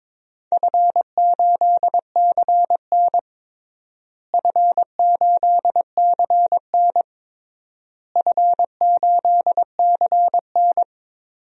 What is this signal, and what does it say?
Morse code, heard as a keyed tone, about 22 wpm, 700 Hz. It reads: F8CN F8CN F8CN
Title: DXCC-100 Call Signs - 4 Characters - Repeated 3x - 22wpm